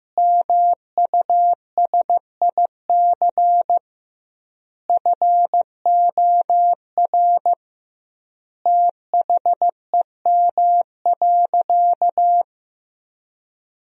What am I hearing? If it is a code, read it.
Morse code, heard as a keyed tone, about 16 wpm, 700 Hz. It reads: MUSIC FOR THEM.
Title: 2x Word-Spacing - Sentences from Top 300 Words - 15wpm